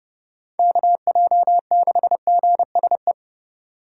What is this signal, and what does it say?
Morse code, heard as a keyed tone, about 27 wpm, 700 Hz. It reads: KJ6GSE